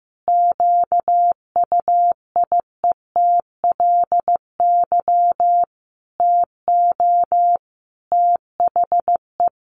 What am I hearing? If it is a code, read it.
Morse code, heard as a keyed tone, about 15 wpm, 700 Hz. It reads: QUIETLY TO THE
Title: A Sound Of Thunder 15wpm